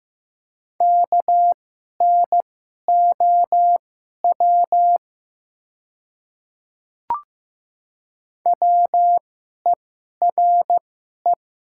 Morse code, heard as a keyed tone, about 15 wpm, spaced as for 11 wpm, 700 Hz. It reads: KNOW WERE
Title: 2x Character Spacing - Top 300 Words - 15wpm